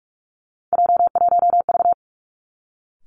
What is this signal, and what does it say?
Morse code, heard as a keyed tone, about 46 wpm, 700 Hz. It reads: J1V
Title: Three Challenging Letter-Number Combinations 45wpm